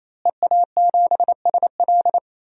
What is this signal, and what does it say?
Morse code, heard as a keyed tone, about 28 wpm, 700 Hz. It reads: EA7SL
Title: DXCC-100 Call Signs - 5 Characters - Repeated 3x - 28wpm